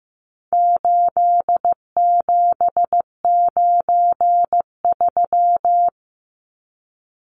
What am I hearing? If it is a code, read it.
Morse code, heard as a keyed tone, about 15 wpm, 700 Hz. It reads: 8793
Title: Numbers - 4-Digit 15wpm